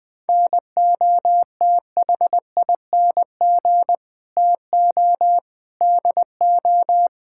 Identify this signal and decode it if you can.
Morse code, heard as a keyed tone, about 20 wpm, 700 Hz. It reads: NOTHING TO DO
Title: Build A Phrase - 4-Word Phrases - 20wpm